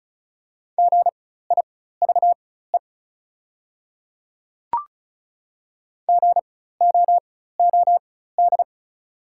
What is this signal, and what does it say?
Morse code, heard as a keyed tone, about 36 wpm, spaced as for 16 wpm, 690 Hz. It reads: GIVE GOOD
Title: 4x Character Spacing - Top 300 Words - 35wpm